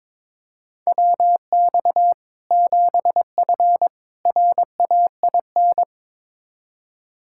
Morse code, heard as a keyed tone, about 22 wpm, 700 Hz. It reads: WX 7F RAIN
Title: Practice QSO - Exchange 2 - Info - 22wpm